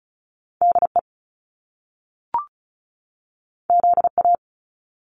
Morse code, heard as a keyed tone, about 36 wpm, 700 Hz. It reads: DE ZA
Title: Top Two Letter Combinations in Words 35wpm